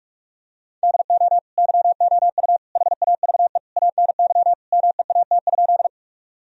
Morse code, heard as a keyed tone, about 41 wpm, 700 Hz. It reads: DO YOU HAVE ANY MEAT?
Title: Sentences from Top 1000 Words - Review Part 3 - 45wpm